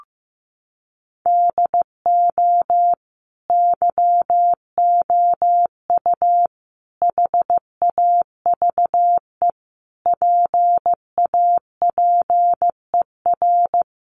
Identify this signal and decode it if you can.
Morse code, heard as a keyed tone, about 15 wpm, 700 Hz. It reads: DO YOU HAVE PAPER
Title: Sentences from Top 1500 Words 15wpm